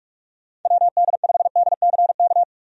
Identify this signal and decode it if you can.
Morse code, heard as a keyed tone, about 46 wpm, 700 Hz. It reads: WD5DCK
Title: USA Call Signs - 6 Characters - 45wpm